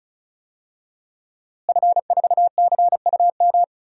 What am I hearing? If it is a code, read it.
Morse code, heard as a keyed tone, about 34 wpm, 700 Hz. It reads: F4CUM
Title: DXCC-100 Call Signs - 5 Characters - Repeated 2x - 35wpm